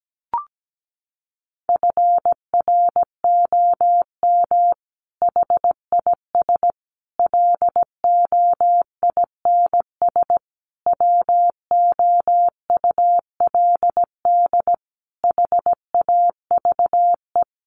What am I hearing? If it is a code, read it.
Morse code, heard as a keyed tone, about 17 wpm, 700 Hz. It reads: FROM HIS LOINS WOULD HAVE
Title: A Sound Of Thunder 17wpm